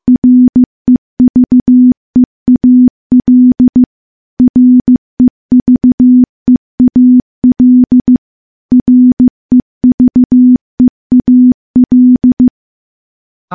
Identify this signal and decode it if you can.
Morse code, heard as a keyed tone, about 15 wpm, 250 Hz. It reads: REVEAL REVEAL REVEAL
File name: morse-training-15WPM